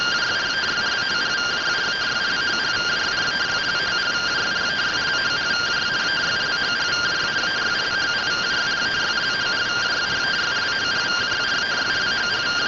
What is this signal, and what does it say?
Two-tone teleprinter signal, about 170 Hz shift.